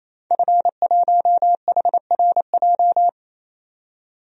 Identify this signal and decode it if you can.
Morse code, heard as a keyed tone, about 28 wpm, 700 Hz. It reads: F1HRJ
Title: DXCC-100 Call Signs - 5 Characters - 28wpm